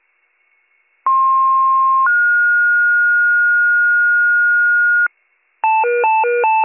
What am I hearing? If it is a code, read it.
Two-tone page, heard as a one-second tone, then a three-second tone, about 1060 then 1430 Hz, with noise. It is heard from 1.1 to 5.1 s.